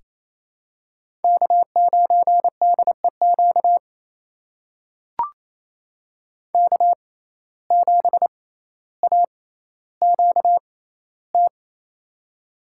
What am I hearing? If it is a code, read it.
Morse code, heard as a keyed tone, about 28 wpm, spaced as for 14 wpm, 700 Hz. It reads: K9DEQ K7AQT
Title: USA Call Signs - 5 Characters - 6x Character Spacing - 28wpm